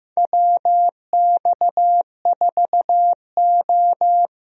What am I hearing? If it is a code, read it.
Morse code, heard as a keyed tone, about 15 wpm, 700 Hz. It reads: WX4O
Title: USA Call Signs - 4 Characters - Repeated 2x - 15wpm